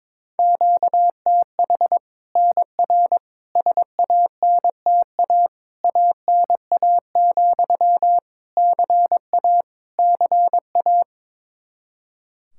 Morse code, heard as a keyed tone, about 22 wpm, 700 Hz. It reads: QTH NR SANTA ANA, CA CA
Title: Practice QSO - Exchange 1 - Beginning - 22wpm